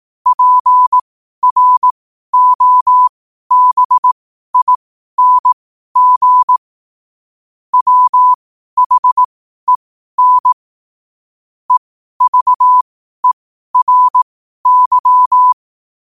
Morse code, heard as a keyed tone, about 18 wpm, spaced as for 12 wpm, 1000 Hz. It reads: PROBING WHEN EVERY
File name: qotdfiles / 12wpm